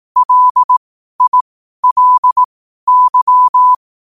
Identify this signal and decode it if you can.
Morse code, heard as a keyed tone, about 18 wpm, spaced as for 12 wpm, 1000 Hz. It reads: LILY
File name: qotdfiles / 12wpm